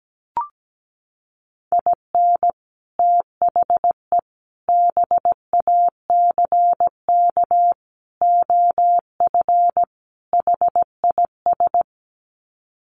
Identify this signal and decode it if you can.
Morse code, heard as a keyed tone, about 17 wpm, 700 Hz. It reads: IN THE BACK OF HIS
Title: Sets of 5 Words Encore2 17wpm